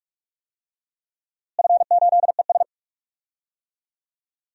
Morse code, heard as a keyed tone, about 47 wpm, 700 Hz. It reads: F8ES
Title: DXCC-100 Call Signs - 4 Characters - Repeated 2x - 45wpm